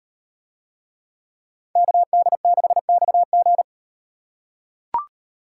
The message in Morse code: KD6XG

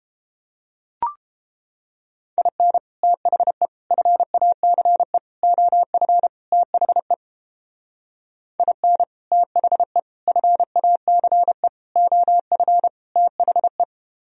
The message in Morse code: IN THE FACE OF THE IN THE FACE OF THE

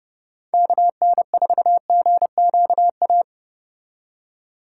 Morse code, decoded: KN4GQA

